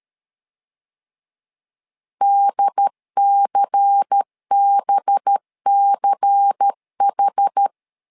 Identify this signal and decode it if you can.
Morse code, heard as a keyed tone, about 13 wpm, 800 Hz. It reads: DCBCH
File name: LEZIONI / 5 LBVC / 5 LBVC 12wpm 1955